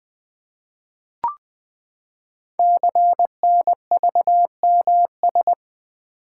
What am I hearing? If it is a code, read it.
Morse code, heard as a keyed tone, about 20 wpm, 700 Hz. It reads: CNVMS
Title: Code Groups - 5 Letters - 20wpm